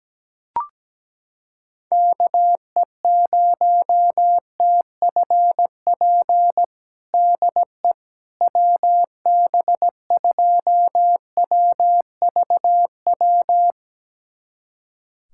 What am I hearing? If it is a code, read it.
Morse code, heard as a keyed tone, about 17 wpm, 700 Hz. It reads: KE0TFP DE WB2WVW
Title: USA Callsign DE Callsign - 6 Characters - 17wpm